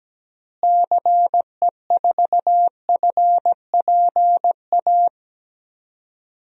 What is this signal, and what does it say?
Morse code, heard as a keyed tone, about 17 wpm, 700 Hz. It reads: CE4FPA
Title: DXCC-100 Call Signs - 6 Characters - Repeated 2x - 17wpm